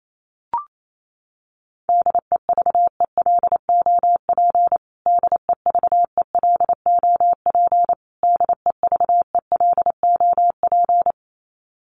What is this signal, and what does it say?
Morse code, heard as a keyed tone, about 28 wpm, 700 Hz. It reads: DEVELOP DEVELOP DEVELOP
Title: Top 500 Words Repeated 28wpm